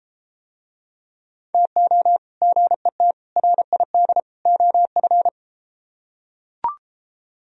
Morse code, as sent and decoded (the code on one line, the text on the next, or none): - --- / --. - / .-. .. -.. / --- ..-.
TO GET RID OF